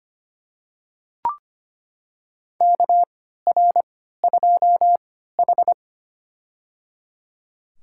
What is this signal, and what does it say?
Morse code, heard as a keyed tone, about 25 wpm, spaced as for 14 wpm, 700 Hz. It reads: KR2H
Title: USA Call Signs - 4 Characters - 3x Character Spacing - 25wpm